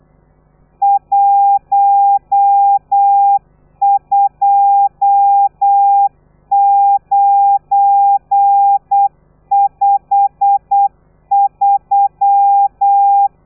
Morse code, heard as a keyed tone, about 8 wpm, 800 Hz. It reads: 12953
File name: telegrafia / i2tfj / Lez12 35